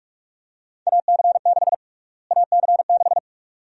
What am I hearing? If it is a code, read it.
Morse code, heard as a keyed tone, about 47 wpm, 700 Hz. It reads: AC6 AC6
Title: USA Call Sign Prefixes - Repeated 2x - 45wpm